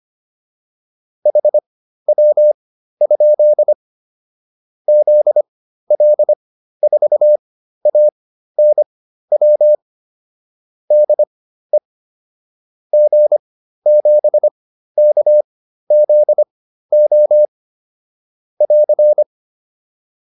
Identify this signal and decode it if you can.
Morse code, heard as a keyed tone, about 25 wpm, spaced as for 13 wpm, 600 Hz. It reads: HW? ZL4ANW DE G7KZO <AR>